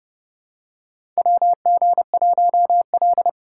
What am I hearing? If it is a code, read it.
Morse code, heard as a keyed tone, about 30 wpm, 700 Hz. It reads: WG1L
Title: USA Call Signs - 4 Characters - 3x Character Spacing - 30wpm